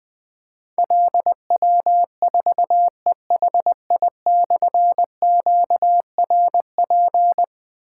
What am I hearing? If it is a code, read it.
Morse code, heard as a keyed tone, about 21 wpm, 700 Hz. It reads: LW4EHI/QRP
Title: Difficult Call Signs 20wpm